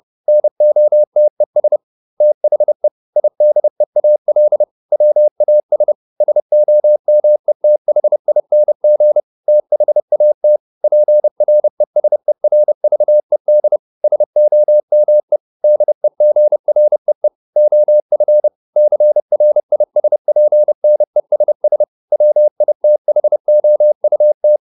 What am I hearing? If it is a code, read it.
Morse code, heard as a keyed tone, about 30 wpm, 600 Hz. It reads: NOTES THE IDEAL WAS SOMETHING THAT PRESERVED SOME DEGREE OF CRISPNESS WITHOUT